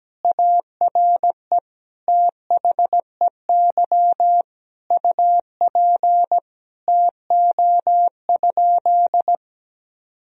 Morse code, heard as a keyed tone, about 17 wpm, 700 Hz. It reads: ARE THEY UP TO?
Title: Sentences from Top 1000 Words - Review Part 1 - 17wpm